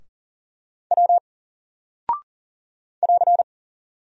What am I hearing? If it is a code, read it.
Morse code, heard as a keyed tone, about 40 wpm, 700 Hz. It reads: W <AR>
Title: Single Character (Letter:Number:Punctuation:Prosign) 40wpm